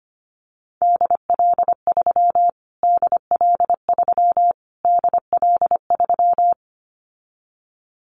Morse code, heard as a keyed tone, about 25 wpm, 700 Hz. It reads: DL3 DL3 DL3
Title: Callsign Prefixes for Top 25 DXCC - Repeated 3x - 25wpm